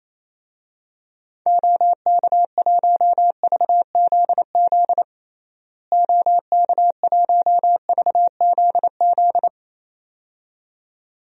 Morse code, heard as a keyed tone, about 27 wpm, 700 Hz. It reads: OK1VZZ OK1VZZ
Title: DXCC-100 Call Signs - 6 Characters - Repeated 2x - 28wpm